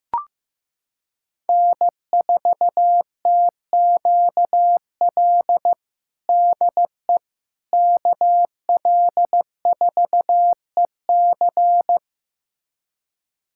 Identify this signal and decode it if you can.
Morse code, heard as a keyed tone, about 15 wpm, 700 Hz. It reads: N4TQL DE KL4EC
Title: USA Callsign DE Callsign - 5 Characters - 15wpm